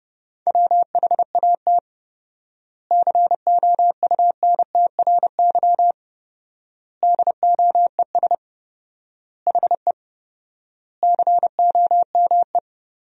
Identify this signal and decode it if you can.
Morse code, heard as a keyed tone, about 30 wpm, 700 Hz. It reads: WHAT COUNTRY DOES HE COME
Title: 4x Word-Spacing - Sentences from Top 300 Words - 30wpm